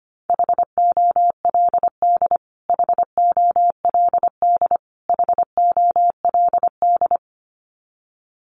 Morse code, heard as a keyed tone, about 25 wpm, 700 Hz. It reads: HOLD HOLD HOLD
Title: Top 500 Words Repeated 25wpm